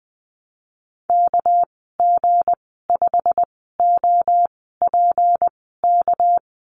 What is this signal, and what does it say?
Morse code, heard as a keyed tone, about 20 wpm, spaced as for 15 wpm, 700 Hz. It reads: KG5OPK
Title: 213 Lesson 34g - Number 8 - Call Signs 20wpm